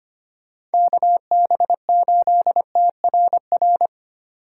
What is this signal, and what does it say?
Morse code, heard as a keyed tone, about 29 wpm, 700 Hz. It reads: KB8TRR
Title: Speed Racing - USA 2x3 Call Signs 25wpm